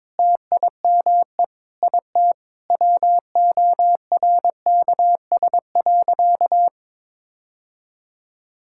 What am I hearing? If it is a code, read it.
Morse code, heard as a keyed tone, about 22 wpm, 700 Hz. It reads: TIME IT WORKS.
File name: practice / 22wpm / Sentences with 3-Word Phrases - 22wpm